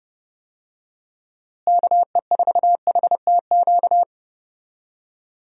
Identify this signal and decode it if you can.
Morse code, heard as a keyed tone, about 30 wpm, 700 Hz. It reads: KE4HTQ
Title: USA Call Signs - 6 Characters - Repeated 2x - 30wpm